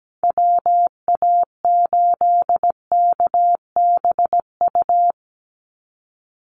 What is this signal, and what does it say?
Morse code, heard as a keyed tone, about 17 wpm, 700 Hz. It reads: WA8KBU
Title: Call Signs Repeated 2x - 17wpm